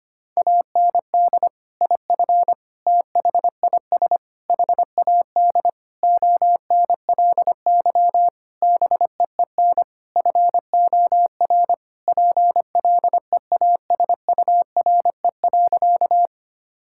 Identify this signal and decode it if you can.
Morse code, heard as a keyed tone, about 25 wpm, 700 Hz. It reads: AND IF THIS HAD ONLY BEEN FOR PLEASURE.
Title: Sentences from Top 1600 Words 25wpm